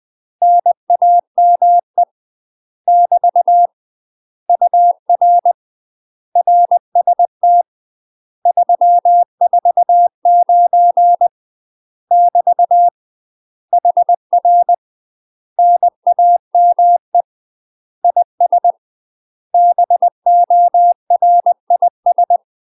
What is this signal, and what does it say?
Morse code, heard as a keyed tone, about 20 wpm, 700 Hz. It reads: NAME <BT> UR RST 349 <BT> HR NAME IS BORIS
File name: UK MASTER QSOs TRAINING / 20WPM / UK MASTER QSOs 0063